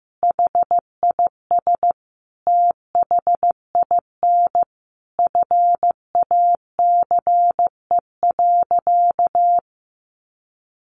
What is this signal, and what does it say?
Morse code, heard as a keyed tone, about 15 wpm, 700 Hz. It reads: HIS THIN FACE.
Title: A Sound Of Thunder 15wpm